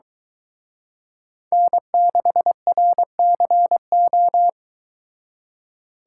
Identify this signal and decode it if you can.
Morse code, heard as a keyed tone, about 22 wpm, 700 Hz. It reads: N6RCO